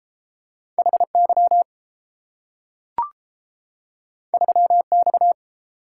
Morse code, heard as a keyed tone, about 32 wpm, 700 Hz. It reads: H Y 3 X